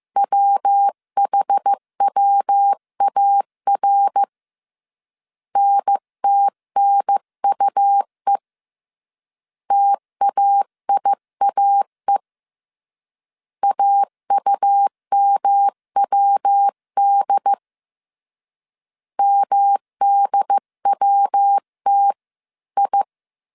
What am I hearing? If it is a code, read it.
Morse code, heard as a keyed tone, about 15 wpm, 800 Hz. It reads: WHWAR NTNUE TAIAE AUMWD MDWTI
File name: CW / DET / 1 3 1500C S C 14wpm 2458